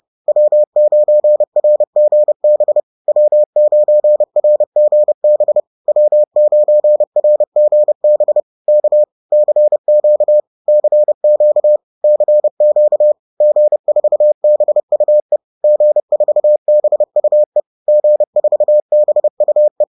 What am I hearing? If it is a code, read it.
Morse code, heard as a keyed tone, about 30 wpm, 600 Hz. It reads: W9RGB W9RGB W9RGB K CQ CQ CQ G4BUE G4BUE G4BUE
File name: 30wpm